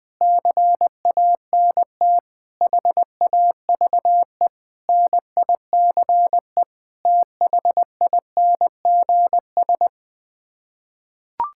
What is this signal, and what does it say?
Morse code, heard as a keyed tone, about 20 wpm, 700 Hz. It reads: CANT HAVE NICE THINGS
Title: Sets of 8 Words - 20wpm